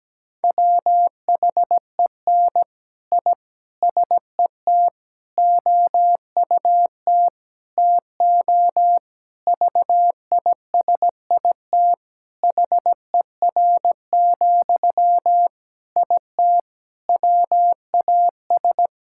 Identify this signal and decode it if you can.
Morse code, heard as a keyed tone, about 17 wpm, 700 Hz. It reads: WHEN I SET OUT TO VISIT HER, IT WAS
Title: Sentences from Top 1000 Words - Review Part 6 - 17wpm